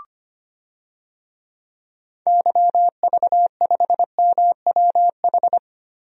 Morse code, heard as a keyed tone, about 25 wpm, 700 Hz. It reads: YV5MWH
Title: DXCC-100 Call Signs - 6 Characters - Repeated 2x - 25wpm